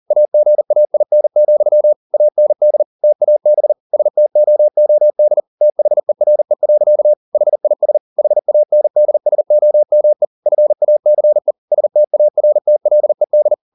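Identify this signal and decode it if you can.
Morse code, heard as a keyed tone, about 40 wpm, 600 Hz. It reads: AGAIN, AND TAB STOOD THERE. HIS HANDSOME FACE STARTLED
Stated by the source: Morse code at 40 wpm, practice text, CW med Gnister.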